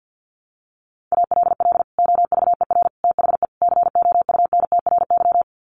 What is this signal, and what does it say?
Morse code, heard as a keyed tone, about 50 wpm, 700 Hz. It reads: ALL OVER THE COUNTRY
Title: Sets of 4 Words 50wpm